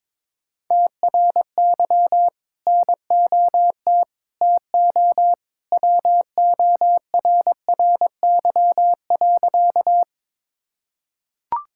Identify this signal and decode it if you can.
Morse code, heard as a keyed tone, about 22 wpm, 700 Hz. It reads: TRY NOT TO WORRY.
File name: practice / 22wpm / Sentences with 2-Word Phrases - 22wpm